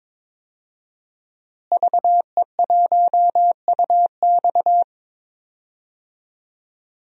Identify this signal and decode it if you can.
Morse code, heard as a keyed tone, about 22 wpm, 700 Hz. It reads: VE1UX
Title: DXCC-100 Call Signs - 5 Characters - Repeated 3x - 22wpm